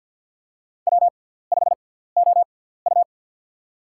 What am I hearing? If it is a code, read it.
Morse code, heard as a keyed tone, about 53 wpm, spaced as for 17 wpm, 700 Hz. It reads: W5OU
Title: USA Call Signs - 4 Characters - 6x Character Spacing - 50wpm